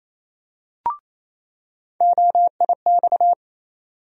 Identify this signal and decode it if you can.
Morse code, heard as a keyed tone, about 28 wpm, 700 Hz. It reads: OIX
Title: USA Call Sign Postfixes - 3 Letters 28wpm